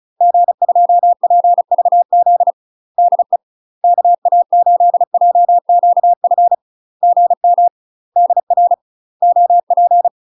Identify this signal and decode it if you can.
Morse code, heard as a keyed tone, about 35 wpm, 700 Hz. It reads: G2PVZ DE KA8JQF GM DR OP
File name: UK MASTER QSOs TRAINING / 35WPM / UK MASTER QSOs 0022